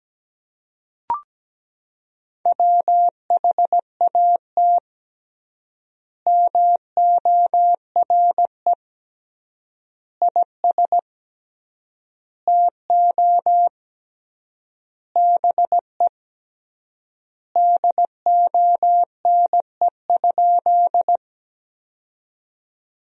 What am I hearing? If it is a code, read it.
Morse code, heard as a keyed tone, about 17 wpm, 700 Hz. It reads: WHAT MORE IS TO BE DONE?
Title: 3x Word-Spacing - Sentences from Top 500 Words - 17wpm